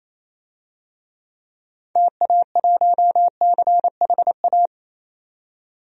Morse code, heard as a keyed tone, about 28 wpm, 700 Hz. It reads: TA1CHA
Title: DXCC-100 Call Signs - 6 Characters - 28wpm